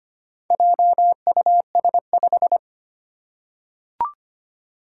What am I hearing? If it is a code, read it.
Morse code, heard as a keyed tone, about 25 wpm, 700 Hz. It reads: JUS5
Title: Four Challenging Letter-Number Combinations 25wpm